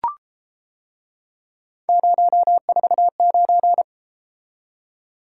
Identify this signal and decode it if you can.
Morse code, heard as a keyed tone, about 31 wpm, 700 Hz. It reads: 049